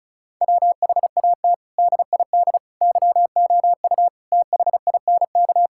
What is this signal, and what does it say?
Morse code, heard as a keyed tone, about 34 wpm, 700 Hz. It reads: WHAT DID YOU THINK
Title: Sentences from Top 1000 Words - Review Part 6 - 35wpm